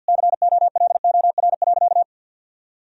Morse code, heard as a keyed tone, about 49 wpm, 700 Hz. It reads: COLOR.
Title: Sentences from Top 1000 Words - Review Part 2 - 50wpm